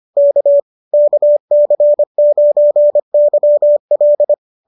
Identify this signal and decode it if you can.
Morse code, heard as a keyed tone, about 25 wpm, 570 Hz. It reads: K KC9YL